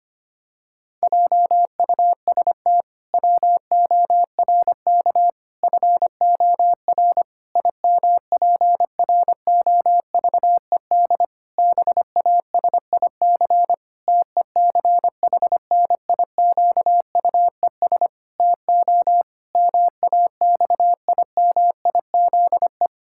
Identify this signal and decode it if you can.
Morse code, heard as a keyed tone, about 25 wpm, 700 Hz. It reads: JUST WORK FOR IMPROVED BASIC TECHNIQUES TO MAXIMIZE
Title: Pangram Challenge v2 - 25wpm